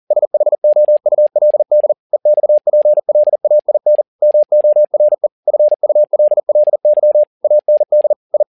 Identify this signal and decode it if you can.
Morse code, heard as a keyed tone, about 40 wpm, 600 Hz. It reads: SHOULD EXPLAIN MORE FULLY AND I
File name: learning center / tower trouble / tower touble 40wpm